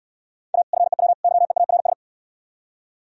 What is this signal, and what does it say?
Morse code, heard as a keyed tone, about 70 wpm, 700 Hz. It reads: A FEW YEARS